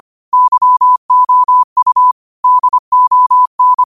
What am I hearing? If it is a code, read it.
Morse code, heard as a keyed tone, about 25 wpm, 1000 Hz. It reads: YOU DON
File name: qotdfiles / 25wpm